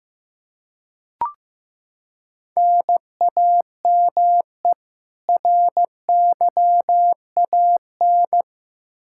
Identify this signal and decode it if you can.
Morse code, heard as a keyed tone, about 15 wpm, 700 Hz. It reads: NAME RYAN
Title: K1USN SST QSO Practice 15wpm